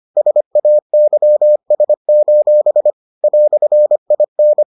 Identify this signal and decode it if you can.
Morse code, heard as a keyed tone, about 25 wpm, 600 Hz. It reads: SAYS: "IN